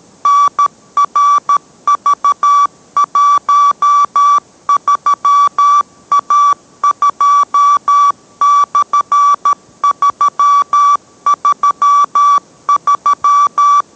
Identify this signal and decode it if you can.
Morse code, heard as a keyed tone, about 13 wpm, 1150 Hz. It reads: NRV13A2/333